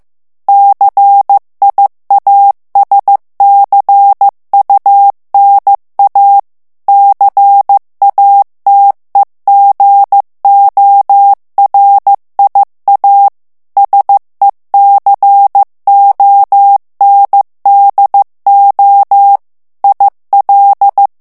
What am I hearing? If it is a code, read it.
Morse code, heard as a keyed tone, about 15 wpm, 800 Hz. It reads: CIASCUNA CATEGORIA SECONDO IL
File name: morse mp3 / ITALIEN texte clair 2 vitesse 14 mots minute